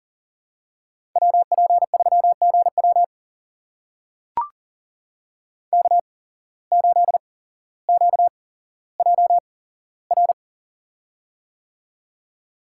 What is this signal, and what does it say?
Morse code, heard as a keyed tone, about 39 wpm, spaced as for 17 wpm, 700 Hz. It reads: WP3GW K8QJR
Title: USA Call Signs - 5 Characters - 8x Character Spacing - 40wpm